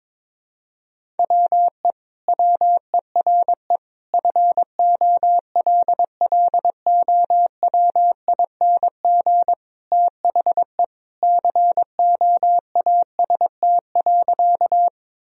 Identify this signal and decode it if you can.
Morse code, heard as a keyed tone, about 22 wpm, 700 Hz. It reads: WE WERE FOLLOWING THE COAST.
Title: Sentences from Top 1000 Words - Review Part 4 - 22wpm